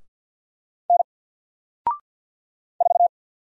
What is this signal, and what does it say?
Morse code, heard as a keyed tone, about 50 wpm, 700 Hz. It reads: N 4
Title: Single Letter-Number 50wpm